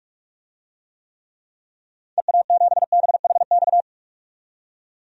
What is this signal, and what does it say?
Morse code, heard as a keyed tone, about 46 wpm, 700 Hz. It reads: EA7BHX
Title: DXCC-100 Call Signs - 6 Characters - 45wpm